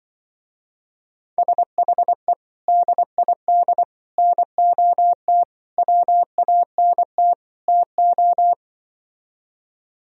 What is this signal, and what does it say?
Morse code, heard as a keyed tone, about 24 wpm, 700 Hz. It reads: SHE DID NOT WANT TO
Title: Speed-Racing - Sets of 5 Words 20wpm